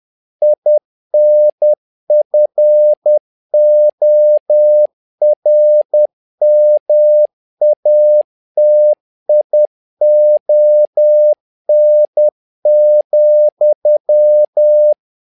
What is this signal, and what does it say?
Morse code, heard as a keyed tone, about 10 wpm, 600 Hz. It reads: INFORMATION,
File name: learning center / tower trouble / tower trouble 10wpm